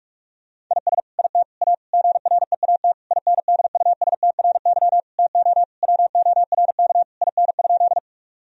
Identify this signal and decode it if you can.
Morse code, heard as a keyed tone, about 46 wpm, 700 Hz. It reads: IS IT A GREAT INDUSTRY TO WORK IN?
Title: Sentences from Top 1000 Words - Review Part 4 - 45wpm